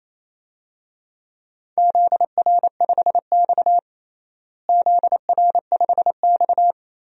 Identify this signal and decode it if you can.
Morse code, heard as a keyed tone, about 28 wpm, 700 Hz. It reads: ZR5X ZR5X
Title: DXCC-100 Call Signs - 4 Characters - Repeated 3x - 28wpm